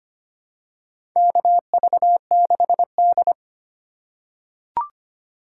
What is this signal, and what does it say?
Morse code, heard as a keyed tone, about 25 wpm, 700 Hz. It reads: KV6D
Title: USA Call Signs - 3x Character Spacing - 25wpm